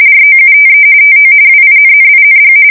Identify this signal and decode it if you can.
RTTY, about 45.45 baud. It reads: ROUTE TO WINNING
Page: 170 Shift 2125/2295